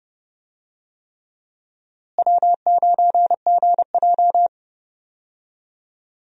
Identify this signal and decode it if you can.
Morse code, heard as a keyed tone, about 30 wpm, 700 Hz. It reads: W9GJ
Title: USA Call Signs - 4 Characters - Repeated 3x - 30wpm